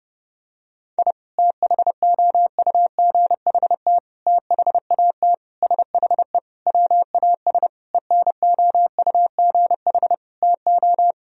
Morse code, heard as a keyed tone, about 30 wpm, 700 Hz. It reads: I THOUGHT THAT SHE WAS ENOUGH TO
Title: Sentences from Top 1000 Words - Review Part 3 - 30wpm